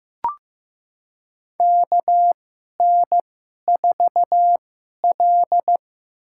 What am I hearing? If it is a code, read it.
Morse code, heard as a keyed tone, about 15 wpm, spaced as for 11 wpm, 700 Hz. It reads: KN4L
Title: USA Call Signs - 4 Characters - 2x Character Spacing - 15wpm